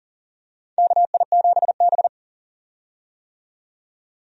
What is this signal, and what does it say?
Morse code, heard as a keyed tone, about 40 wpm, 700 Hz. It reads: KI7B
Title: USA Call Signs - 4 Characters - 40wpm